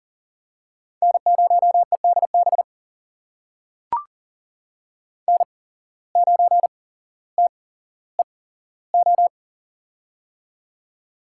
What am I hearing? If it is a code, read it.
Morse code, heard as a keyed tone, about 38 wpm, spaced as for 17 wpm, 700 Hz. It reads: N0EDB N9TEO
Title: USA Call Signs - 5 Characters - 8x Character Spacing - 40wpm